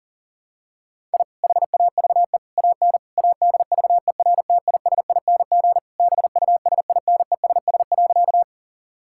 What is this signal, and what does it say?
Morse code, heard as a keyed tone, about 37 wpm, 700 Hz. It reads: I HAVE AN ADVERTISING BUSINESS.
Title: Sentences from Top 1000 Words - Review Part 6 - 40wpm